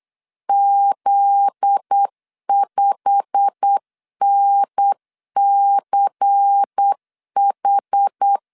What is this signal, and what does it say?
Morse code, heard as a keyed tone, about 8 wpm, 800 Hz. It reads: Z5NCH
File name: 8wpm